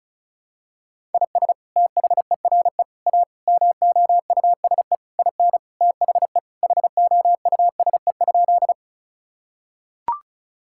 Morse code, heard as a keyed tone, about 36 wpm, 700 Hz. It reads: IS THERE A MOUSE IN THE HOUSE?
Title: Sentences from Top 1000 Words - Review Part 5 - 35wpm